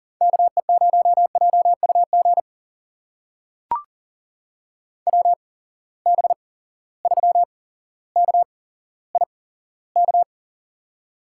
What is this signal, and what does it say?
Morse code, heard as a keyed tone, about 37 wpm, spaced as for 17 wpm, 700 Hz. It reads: KE0JUG WB3KIK
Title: USA Call Signs - 6 Characters - 8x Character Spacing - 40wpm